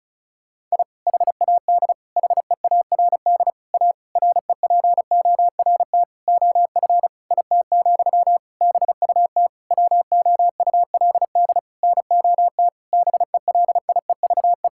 Morse code, heard as a keyed tone, about 36 wpm, 700 Hz. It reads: I HAD HEARD A REPORT OF IT, BUT WOULD NOT BELIEVE
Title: Sentences from Top 1000 Words - Review Part 6 - 35wpm